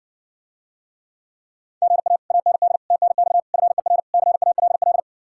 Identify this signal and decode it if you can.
Morse code, heard as a keyed tone, about 60 wpm, 700 Hz. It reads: GA AND TNX FER CALL